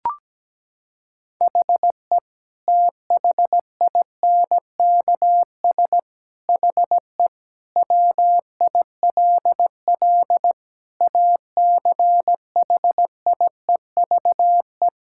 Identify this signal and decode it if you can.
Morse code, heard as a keyed tone, about 17 wpm, 700 Hz. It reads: HE THINKS HE WILL ACHIEVE
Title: Sentences from Top 1000 Words - Review Part 6 - 17wpm